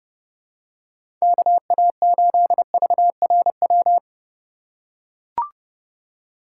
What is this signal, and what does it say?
Morse code, heard as a keyed tone, about 30 wpm, 700 Hz. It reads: KA8VRW
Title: USA Call Signs - 6 Characters - 2x Character Spacing - 30wpm